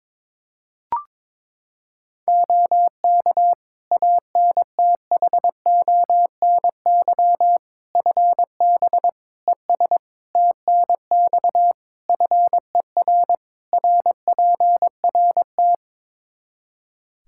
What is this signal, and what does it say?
Morse code, heard as a keyed tone, about 22 wpm, 700 Hz. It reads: OK ANTHONY FB ES TNX FER RPRT
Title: Practice QSO - Exchange 2 - Info - 22wpm